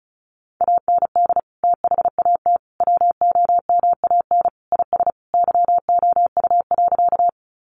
Morse code, heard as a keyed tone, about 34 wpm, 700 Hz. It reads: AND THAT WOMAN IS YOU.
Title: Sentences from Top 1100 Words 35wpm